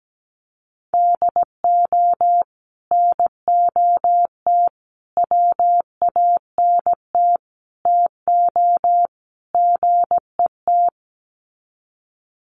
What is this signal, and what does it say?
Morse code, heard as a keyed tone, about 17 wpm, 700 Hz. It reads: DO NOT WANT TO GET
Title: Sets of 5 Words 17wpm